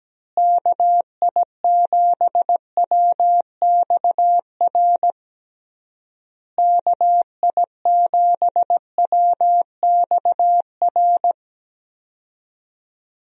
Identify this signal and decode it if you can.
Morse code, heard as a keyed tone, about 17 wpm, 700 Hz. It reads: KI7WXR KI7WXR
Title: USA Call Signs - 6 Characters - Repeated 3x - 17wpm